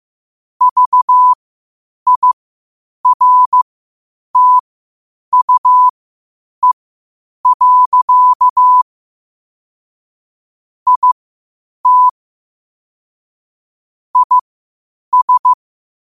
Morse code, heard as a keyed tone, about 15 wpm, spaced as for 8 wpm, 1000 Hz. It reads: VIRTUE. IT IS
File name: qotdfiles / 8wpm